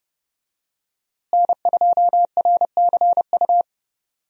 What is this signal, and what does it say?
Morse code, heard as a keyed tone, about 32 wpm, 700 Hz. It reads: N2RCU
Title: USA Call Signs - 5 Characters - Repeated 2x - 30wpm